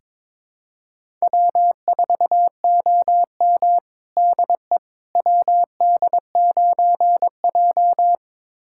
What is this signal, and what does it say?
Morse code, heard as a keyed tone, about 23 wpm, 700 Hz. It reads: W4OM DE WD9J
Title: USA Callsign DE Callsign - 4 Characters - 22wpm